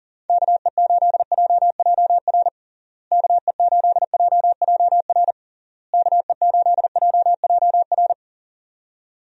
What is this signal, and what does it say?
Morse code, heard as a keyed tone, about 40 wpm, 700 Hz. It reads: KE8JJR KE8JJR KE8JJR
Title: USA Call Signs - 6 Characters - Repeated 3x - 40wpm